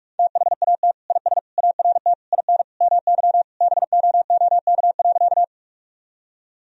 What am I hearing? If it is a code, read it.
Morse code, heard as a keyed tone, about 46 wpm, 700 Hz. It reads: THAT IS ART IN MY BOOK.
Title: Sentences from Top 1000 Words - Review Part 3 - 45wpm